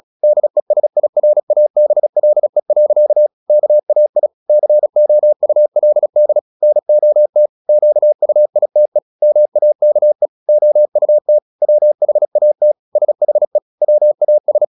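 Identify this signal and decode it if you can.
Morse code, heard as a keyed tone, about 36 wpm, 600 Hz. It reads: DESIRABLE. KAI COULD NOT QUITE MAKE OUT WHAT SHE WAS
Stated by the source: Morse practice at 36 wpm, text of Ken Follett's novel Never, from CW med Gnister.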